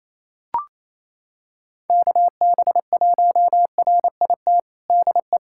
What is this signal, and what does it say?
Morse code, heard as a keyed tone, about 28 wpm, 700 Hz. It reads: KB1RIT DE
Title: USA Callsign DE Callsign - 6 Characters - 28wpm